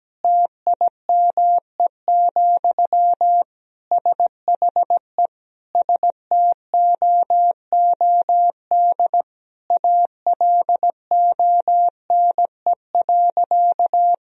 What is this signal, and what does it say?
Morse code, heard as a keyed tone, about 17 wpm, 700 Hz. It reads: TIME, SHE STOOD ALONE.
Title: Sentences from Top 1000 Words - Review Part 2 - 17wpm